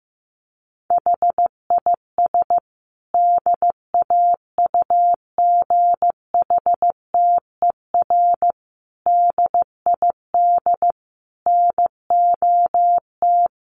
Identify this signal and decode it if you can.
Morse code, heard as a keyed tone, about 14 wpm, 700 Hz. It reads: HIS DAUGHTER DID NOT
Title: Sentences from Top 1500 Words 15wpm